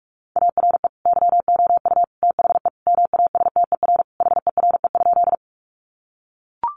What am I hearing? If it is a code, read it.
Morse code, heard as a keyed tone, about 44 wpm, 700 Hz. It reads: ARE YOU THE MASTER HERE?
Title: Sentences from Top 1300 Words 45wpm